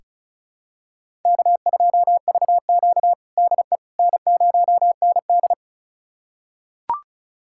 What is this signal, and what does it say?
Morse code, heard as a keyed tone, about 36 wpm, 700 Hz. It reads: K2VQ DE N0ND
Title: USA Callsign DE Callsign - 4 Characters - 35wpm